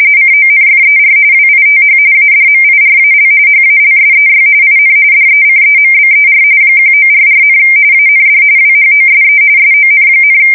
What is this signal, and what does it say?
RTTY, about 45.45 baud. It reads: CONFER WITH ADMIRAL HALSEY IN DECEMBER 1944, AND WHOSE SON,
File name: fsk170 / battleshipnj